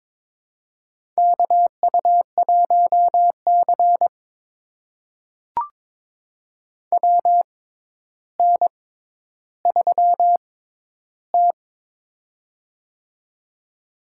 Morse code, heard as a keyed tone, about 22 wpm, spaced as for 11 wpm, 700 Hz. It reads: KU1C WN3T
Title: USA Call Signs - 4 Characters - 6x Character Spacing - 22wpm